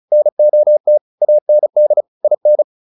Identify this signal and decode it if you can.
Morse code, heard as a keyed tone, about 34 wpm, 600 Hz. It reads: NOT AND IN